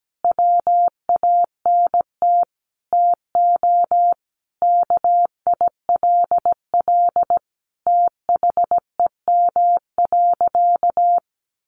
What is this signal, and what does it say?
Morse code, heard as a keyed tone, about 17 wpm, 700 Hz. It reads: WANT TO KILL THEM.
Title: Additional Easy Phrases 17wpm